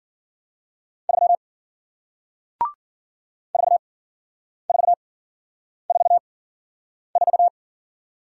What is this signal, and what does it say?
Morse code, heard as a keyed tone, about 53 wpm, spaced as for 9 wpm, 700 Hz. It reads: 3 4444